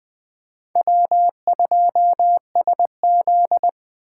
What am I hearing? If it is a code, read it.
Morse code, heard as a keyed tone, about 24 wpm, 700 Hz. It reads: W2SZ
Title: Speed Racing - USA 1x2 Call Signs 20wpm